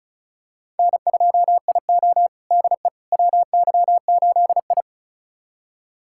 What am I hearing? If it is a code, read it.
Morse code, heard as a keyed tone, about 36 wpm, 700 Hz. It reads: N2IO DE WY8I
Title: USA Callsign DE Callsign - 4 Characters - 35wpm